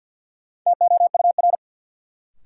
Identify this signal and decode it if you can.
Morse code, heard as a keyed tone, about 48 wpm, 700 Hz. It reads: TOUR
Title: Sentences from Top 1400 Words - New Word Drill - 50wpm